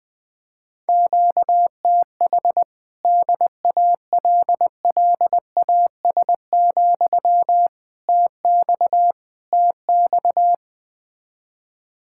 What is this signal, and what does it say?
Morse code, heard as a keyed tone, about 20 wpm, 700 Hz. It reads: QTH DALLAS, TX TX
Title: Speed-Racing Common QSO Phrases Encore - 17wpm